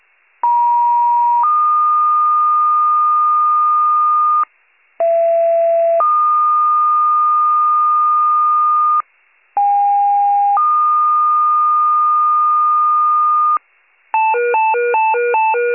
Two-tone page, heard as a one-second tone, then a three-second tone, about 950 then 1220 Hz, about 670 then 1220 Hz, about 810 then 1220 Hz. It is heard from 0.4 to 13.6 s.